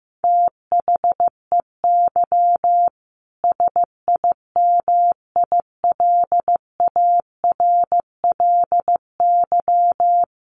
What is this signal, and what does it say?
Morse code, heard as a keyed tone, about 14 wpm, 700 Hz. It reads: THEY SIMILARLY
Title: Sentences from Top 1400 Words 15wpm